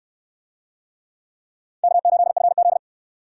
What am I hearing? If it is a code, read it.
Morse code, heard as a keyed tone, about 69 wpm, 700 Hz. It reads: K8FZ